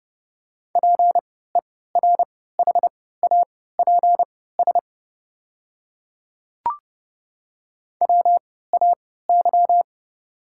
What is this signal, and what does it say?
Morse code, heard as a keyed tone, about 30 wpm, spaced as for 17 wpm, 700 Hz. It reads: PERHAPS WAY
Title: 3x Character Spacing - Top 500 Words - 30wpm